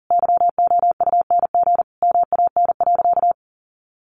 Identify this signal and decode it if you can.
Morse code, heard as a keyed tone, about 40 wpm, 700 Hz. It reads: YOUNG MAN.
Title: Sentences from Top 900 Words 40wpm